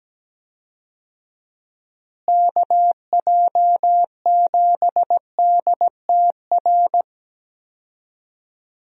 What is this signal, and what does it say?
Morse code, heard as a keyed tone, about 17 wpm, 700 Hz. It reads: KJ7DTR
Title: USA Call Signs - 6 Characters - Repeated 3x - 17wpm